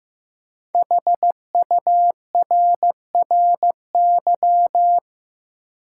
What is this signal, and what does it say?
Morse code, heard as a keyed tone, about 15 wpm, 700 Hz. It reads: HURRY
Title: Five Letter Words - 15wpm